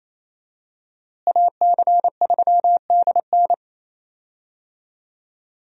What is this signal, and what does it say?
Morse code, heard as a keyed tone, about 27 wpm, 700 Hz. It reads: AC3DN
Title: USA Call Signs - 5 Characters - Repeated 3x - 28wpm